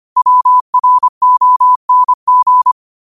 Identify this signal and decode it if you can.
Morse code, heard as a keyed tone, about 25 wpm, 1000 Hz. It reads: WRONG
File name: qotdfiles / 25wpm